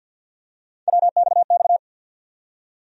Morse code, heard as a keyed tone, about 53 wpm, 700 Hz. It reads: WXX
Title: USA Call Sign Postfixes - 3 Letters 50wpm